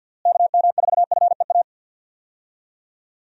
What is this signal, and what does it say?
Morse code, heard as a keyed tone, about 52 wpm, 700 Hz. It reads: KM4FEA
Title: USA Call Signs - 6 Characters - Repeated 3x - 50wpm